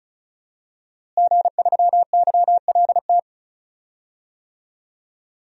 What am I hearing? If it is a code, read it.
Morse code, heard as a keyed tone, about 34 wpm, 700 Hz. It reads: G3YLT
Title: DXCC-100 Call Signs - 5 Characters - Repeated 2x - 35wpm